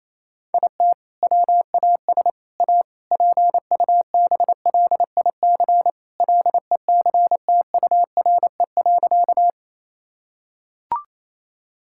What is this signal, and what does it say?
Morse code, heard as a keyed tone, about 28 wpm, 700 Hz. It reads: IT WAS A PUBLIC LECTURE.
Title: Sentences from Top 1600 Words 28wpm